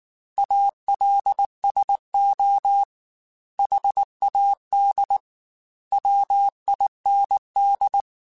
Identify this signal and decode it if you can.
Morse code, heard as a keyed tone, about 19 wpm, 800 Hz. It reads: ALSO HAD WIND
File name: Weather Sets / Weather Sets-mp3s / Weather 5 5 @18